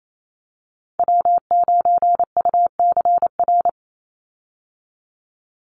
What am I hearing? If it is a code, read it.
Morse code, heard as a keyed tone, about 28 wpm, 700 Hz. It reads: W9UCR
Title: Call Signs Repeated 3x - 28wpm